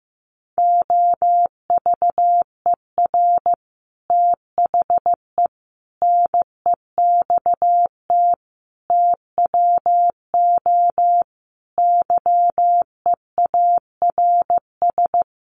Morse code, heard as a keyed tone, about 15 wpm, 700 Hz. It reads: OVER THE NEXT TWO YEARS
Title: Sets of 5 Words Encore2 15wpm